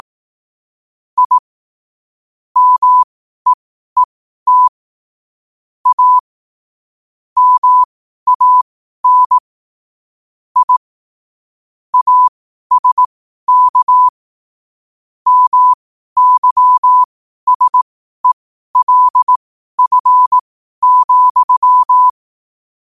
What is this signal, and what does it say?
Morse code, heard as a keyed tone, about 18 wpm, spaced as for 12 wpm, 1000 Hz. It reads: I MEET A MAN I ASK MYSELF,
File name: qotdfiles / 12wpm